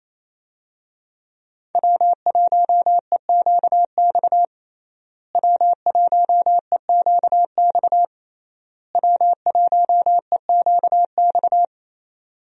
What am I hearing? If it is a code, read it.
Morse code, heard as a keyed tone, about 27 wpm, 700 Hz. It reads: W1EQX W1EQX W1EQX
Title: USA Call Signs - 5 Characters - Repeated 3x - 28wpm